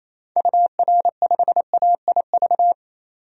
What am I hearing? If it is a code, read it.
Morse code, heard as a keyed tone, about 28 wpm, 700 Hz. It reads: UR5AIV
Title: DXCC-100 Call Signs - 6 Characters - 28wpm